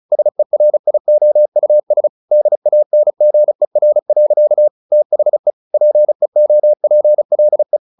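Morse code, heard as a keyed tone, about 34 wpm, 600 Hz. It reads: SERIOUS DANGER. THE PEOPLE
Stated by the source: Morse practice at 35 wpm from CW med Gnister, sending Ken Follett's novel Never in English.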